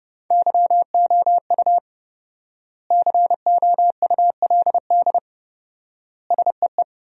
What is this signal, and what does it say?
Morse code, heard as a keyed tone, about 30 wpm, 700 Hz. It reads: YOU COULD SEE
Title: 4x Word-Spacing - Sentences from Top 500 Words - 30wpm